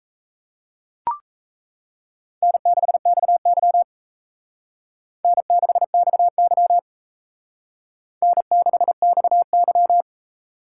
Morse code, heard as a keyed tone, about 34 wpm, 700 Hz. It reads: N6XY N6XY N6XY
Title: Speed Racing - USA Call Signs 28wpm